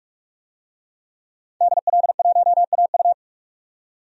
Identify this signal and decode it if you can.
Morse code, heard as a keyed tone, about 46 wpm, 700 Hz. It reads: DL1AU
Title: DXCC-100 Call Signs - 5 Characters - 45wpm